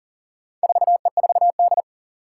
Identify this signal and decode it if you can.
Morse code, heard as a keyed tone, about 40 wpm, 700 Hz. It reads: 4E4D
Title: Four Challenging Letter-Number Combinations 40wpm